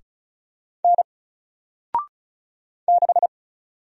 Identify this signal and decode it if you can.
Morse code, heard as a keyed tone, about 36 wpm, 700 Hz. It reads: N 6
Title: Single Letter-Number 35wpm